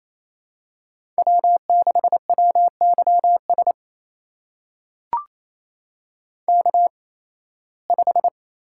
Morse code, heard as a keyed tone, about 28 wpm, spaced as for 18 wpm, 700 Hz. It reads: W6WYS K5
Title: USA Call Signs - 5 Characters - 8x Character Spacing - 28wpm